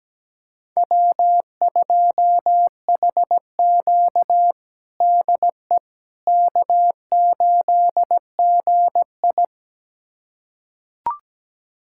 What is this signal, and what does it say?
Morse code, heard as a keyed tone, about 17 wpm, 700 Hz. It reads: W2HQ DE K8GI
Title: USA Callsign DE Callsign - 4 Characters - 17wpm